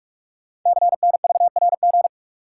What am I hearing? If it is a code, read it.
Morse code, heard as a keyed tone, about 44 wpm, 700 Hz. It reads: CNVRG
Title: Code Groups - 5 Letters - 45wpm